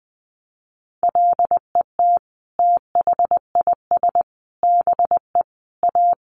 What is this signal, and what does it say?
Morse code, heard as a keyed tone, about 20 wpm, 700 Hz. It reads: LET THIS BE A
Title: Sentences from Top 1200 Words 20wpm